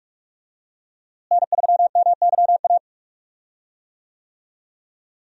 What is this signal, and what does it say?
Morse code, heard as a keyed tone, about 47 wpm, 700 Hz. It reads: N3MYA
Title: USA Call Signs - 5 Characters - Repeated 2x - 45wpm